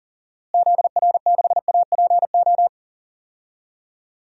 Morse code, heard as a keyed tone, about 40 wpm, 700 Hz. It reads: ZR6APO
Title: DXCC-100 Call Signs - 6 Characters - Repeated 3x - 40wpm